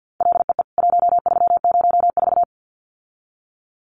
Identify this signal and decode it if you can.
Morse code, heard as a keyed tone, about 50 wpm, 700 Hz. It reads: LEE 1304
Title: CWT Exchange 50wpm